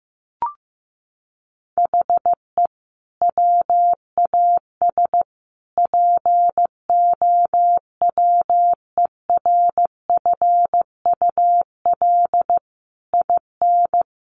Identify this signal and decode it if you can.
Morse code, heard as a keyed tone, about 15 wpm, 700 Hz. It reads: HE WAS POWERFUL IN
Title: Sentences from Top 1300 Words 15wpm